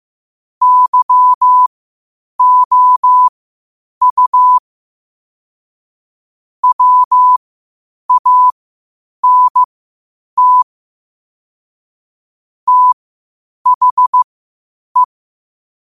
Morse code, heard as a keyed tone, about 15 wpm, spaced as for 8 wpm, 1000 Hz. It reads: YOU WANT THE
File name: qotdfiles / 8wpm